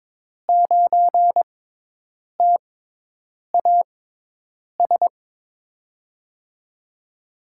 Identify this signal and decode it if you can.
Morse code, heard as a keyed tone, about 22 wpm, spaced as for 8 wpm, 700 Hz. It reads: 9TAS
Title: USA Call Signs - 5 Characters - 6x Character Spacing - 22wpm